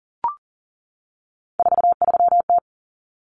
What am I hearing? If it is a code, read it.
Morse code, heard as a keyed tone, about 40 wpm, 700 Hz. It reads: 43T
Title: Common Cut Numbers - 3-Digit 40wpm